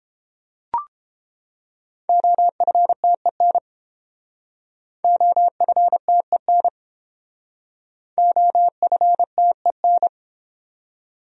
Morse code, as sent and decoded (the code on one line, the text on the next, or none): --- ..-. - . -. / --- ..-. - . -. / --- ..-. - . -.
OFTEN OFTEN OFTEN